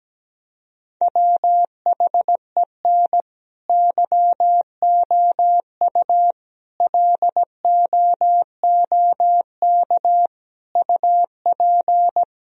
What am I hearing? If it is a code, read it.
Morse code, heard as a keyed tone, about 17 wpm, 700 Hz. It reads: WHEN YOU LOOK UP
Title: Sentences from Top 1000 Words - Review Part 1 - 17wpm